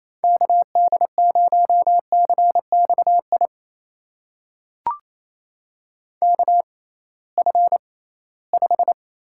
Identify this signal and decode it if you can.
Morse code, heard as a keyed tone, about 28 wpm, spaced as for 18 wpm, 700 Hz. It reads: KD0CXI KF5
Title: USA Call Signs - 6 Characters - 6x Character Spacing - 28wpm